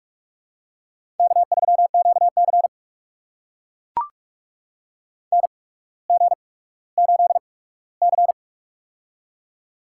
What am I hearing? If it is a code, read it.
Morse code, heard as a keyed tone, about 44 wpm, spaced as for 19 wpm, 700 Hz. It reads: K3QC NG8C
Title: USA Call Signs - 4 Characters - 8x Character Spacing - 45wpm